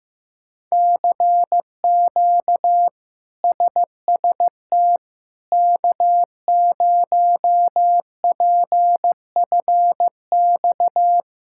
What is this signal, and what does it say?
Morse code, heard as a keyed tone, about 15 wpm, 700 Hz. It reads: CQ SST K0PFX
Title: K1USN SST QSO Practice - 15wpm